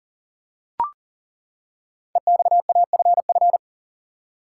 Morse code, heard as a keyed tone, about 40 wpm, 700 Hz. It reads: EXAFF
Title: Code Groups - 5 Letters - 40wpm